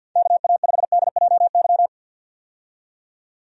Morse code, heard as a keyed tone, about 48 wpm, 700 Hz. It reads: KA5DJY
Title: USA Call Signs - 6 Characters - 50wpm